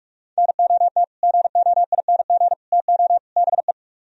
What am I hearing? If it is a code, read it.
Morse code, heard as a keyed tone, about 58 wpm, spaced as for 51 wpm, 700 Hz. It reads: NOT GOING TO BE
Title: Speed-Racing - Sets of 4 Words 45wpm